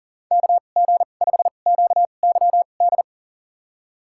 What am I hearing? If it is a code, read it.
Morse code, heard as a keyed tone, about 40 wpm, spaced as for 29 wpm, 700 Hz. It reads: KG5QYD
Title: USA Call Signs - 6 Characters - 2x Character Spacing - 40wpm